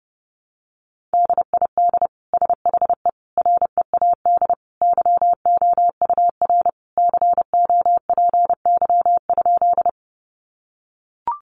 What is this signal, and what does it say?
Morse code, heard as a keyed tone, about 28 wpm, 700 Hz. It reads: DID SHE READ YOUR COPY?
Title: Sentences from Top 1400 Words 30wpm